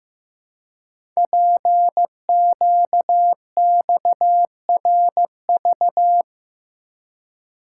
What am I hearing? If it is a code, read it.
Morse code, heard as a keyed tone, about 15 wpm, 700 Hz. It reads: PQXRV
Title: Code Groups - 5 Letters - 15wpm